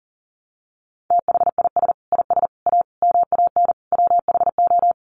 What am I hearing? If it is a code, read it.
Morse code, heard as a keyed tone, about 40 wpm, 700 Hz. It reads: THIS IS A MAN WHO
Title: Sets of 5 Words Encore2 40wpm